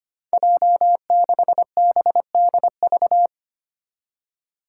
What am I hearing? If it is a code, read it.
Morse code, heard as a keyed tone, about 25 wpm, 700 Hz. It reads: J6BDV
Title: Five Challenging Letter-Number Combinations 25wpm